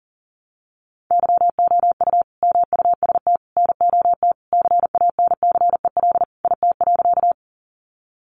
Morse code, heard as a keyed tone, about 40 wpm, 700 Hz. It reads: YOU MUST NOT CANCEL IT.
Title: Sentences from Top 1200 Words 40wpm